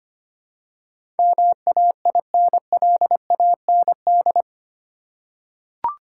Morse code, heard as a keyed tone, about 25 wpm, 700 Hz. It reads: MAINLAND
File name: Eight Letter Words - 25wpm